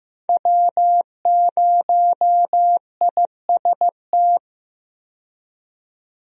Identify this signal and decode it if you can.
Morse code, heard as a keyed tone, about 15 wpm, 700 Hz. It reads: W0IST
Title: USA Call Signs - 5 Characters - 15wpm